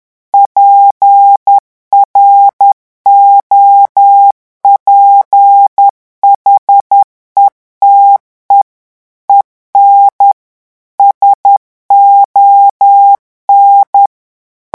Morse code, heard as a keyed tone, about 10 wpm, 800 Hz. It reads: PROPHETE EN SON
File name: morse mp3 / revision francais 10 mots minute